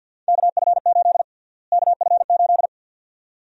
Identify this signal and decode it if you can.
Morse code, heard as a keyed tone, about 46 wpm, 700 Hz. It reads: KF8 KF8
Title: USA Call Sign Prefixes - Repeated 2x - 50wpm